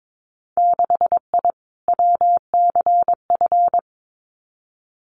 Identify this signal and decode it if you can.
Morse code, heard as a keyed tone, about 22 wpm, 700 Hz. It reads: 6I WCF
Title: Winter Field Day Exchanges Encore 22wpm